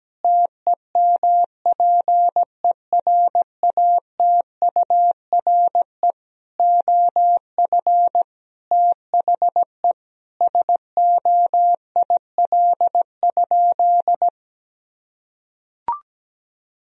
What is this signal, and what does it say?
Morse code, heard as a keyed tone, about 17 wpm, 700 Hz. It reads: TEMPERATURE OF THE SOIL?
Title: Sentences from Top 1000 Words - Review Part 4 - 17wpm